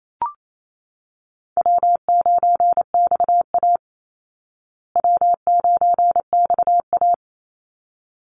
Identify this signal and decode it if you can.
Morse code, heard as a keyed tone, about 28 wpm, 700 Hz. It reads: W9XA W9XA
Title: Easy Call Signs Repeated 2x - 28wpm